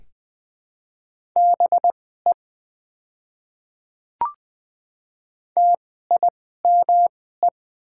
Morse code, heard as a keyed tone, about 20 wpm, spaced as for 14 wpm, 700 Hz. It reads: BE TIME
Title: 2x Character Spacing - Top 100 Words - 20wpm